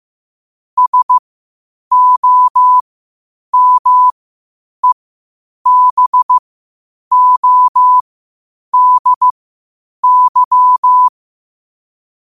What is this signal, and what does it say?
Morse code, heard as a keyed tone, about 15 wpm, spaced as for 8 wpm, 1000 Hz. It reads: SOMEBODY
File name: qotdfiles / 8wpm